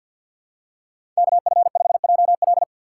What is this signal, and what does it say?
Morse code, heard as a keyed tone, about 51 wpm, 690 Hz. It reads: KF5JL
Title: USA Call Signs - 5 Characters - 3x Character Spacing - 50wpm